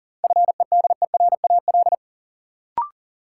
Morse code, heard as a keyed tone, about 40 wpm, 700 Hz. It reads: FEDERAL
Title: Sets of 3 Words - New Word Drill - 40wpm